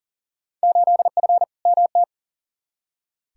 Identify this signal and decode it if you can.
Morse code, heard as a keyed tone, about 38 wpm, 700 Hz. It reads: 8F MT
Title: Field Day Exchanges Encore - 40wpm